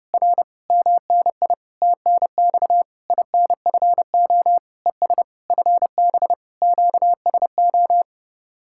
Morse code, heard as a keyed tone, about 33 wpm, 700 Hz. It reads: R MNI TNX INFO ES FB QSO